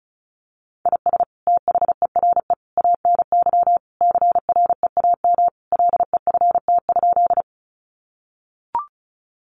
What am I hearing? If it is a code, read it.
Morse code, heard as a keyed tone, about 34 wpm, 700 Hz. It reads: IS THERE ANY CREAM LEFT?
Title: Sentences from Top 1500 Words 35wpm